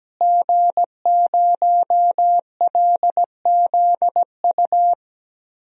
Morse code, heard as a keyed tone, about 17 wpm, 700 Hz. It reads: G0LZU
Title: DXCC-100 Call Signs - 5 Characters - Repeated 3x - 17wpm